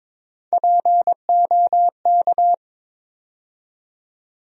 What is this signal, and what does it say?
Morse code, heard as a keyed tone, about 22 wpm, 700 Hz. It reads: POK
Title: USA Call Sign Postfixes - 3 Letters 22wpm